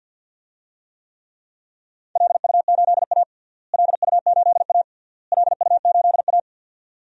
Morse code, heard as a keyed tone, about 52 wpm, 700 Hz. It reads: LU8A LU8A LU8A
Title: DXCC-100 Call Signs - 4 Characters - Repeated 3x - 50wpm